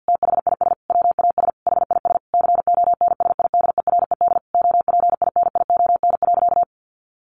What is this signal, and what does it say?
Morse code, heard as a keyed tone, about 50 wpm, 700 Hz. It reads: THIS WAS HIS CONSIDERED OPINION.
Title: A Sound Of Thunder 50wpm